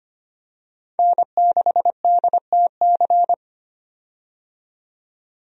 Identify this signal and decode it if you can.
Morse code, heard as a keyed tone, about 25 wpm, 700 Hz. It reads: N6DTC
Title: USA Call Signs - 5 Characters - 25wpm